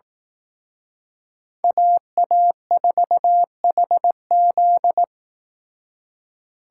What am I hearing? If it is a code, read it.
Morse code, heard as a keyed tone, about 18 wpm, 700 Hz. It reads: AA4HZ
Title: Speed Racing - USA 2x2 Call Signs 15wpm